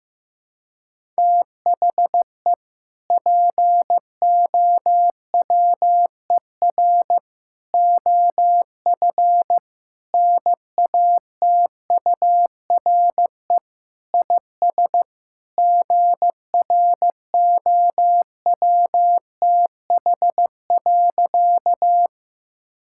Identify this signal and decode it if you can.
Morse code, heard as a keyed tone, about 16 wpm, 700 Hz. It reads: THE POWER OF NATURE IS GROWTH.
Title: Sentences from Top 1000 Words - Review Part 5 - 15wpm